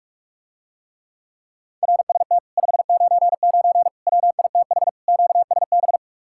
Morse code, heard as a keyed tone, about 48 wpm, 700 Hz. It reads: RST 599 WITH QSB